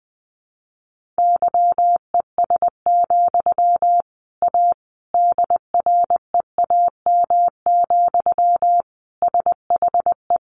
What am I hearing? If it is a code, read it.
Morse code, heard as a keyed tone, about 20 wpm, 700 Hz. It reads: YES, A DREAM, SHE
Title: Sentences from Top 1500 Words 20wpm